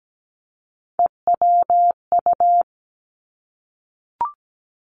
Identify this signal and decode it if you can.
Morse code, heard as a keyed tone, about 17 wpm, 700 Hz. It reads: EWU
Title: Three Challenging Letter-Number Combinations 17wpm